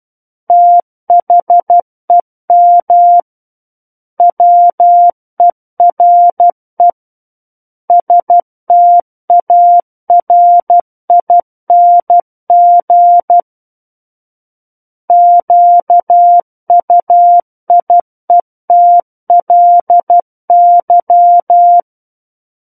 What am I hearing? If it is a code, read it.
Morse code, heard as a keyed tone, about 12 wpm, 700 Hz. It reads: THEM WERE STARING QUIETLY
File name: Book One / 03-Chapter 3 - 12 WPM